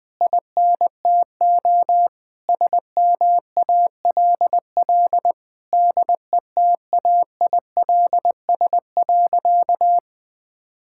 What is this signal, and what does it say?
Morse code, heard as a keyed tone, about 20 wpm, 700 Hz. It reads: INTO SMALL DETAILS.
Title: Sentences with 9-Word Phrases - Spoken Beginning - 20wpm